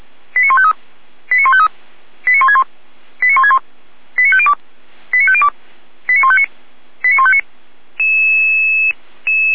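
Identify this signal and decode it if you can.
Five-tone selective call (ZVEI), heard as quick runs of five tones, tones of about 70 ms. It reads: 89153 89153 89161 89161 89502 89502 89159 89159
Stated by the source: ZVEI five-tone call 89160 received by SDR, Feuerwehr Orlamünde.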